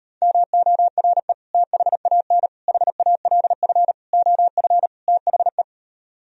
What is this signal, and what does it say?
Morse code, heard as a keyed tone, about 38 wpm, 700 Hz. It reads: MORE THAN HALF OF THE